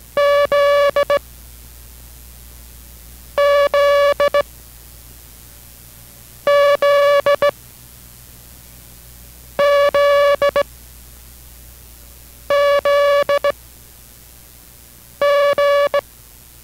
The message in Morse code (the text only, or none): Z Z Z Z Z G